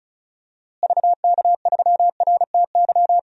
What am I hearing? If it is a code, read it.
Morse code, heard as a keyed tone, about 36 wpm, 700 Hz. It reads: VK3RTY
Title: DXCC-100 Call Signs - 6 Characters - Repeated 3x - 35wpm